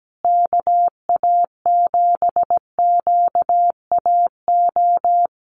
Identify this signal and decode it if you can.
Morse code, heard as a keyed tone, about 17 wpm, 700 Hz. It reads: KA7QAO
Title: USA Call Signs Repeated 2x 17wpm